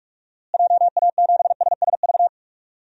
Morse code, heard as a keyed tone, about 41 wpm, 700 Hz. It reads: JA7SSV
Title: DXCC-100 Call Signs - 6 Characters - Repeated 3x - 45wpm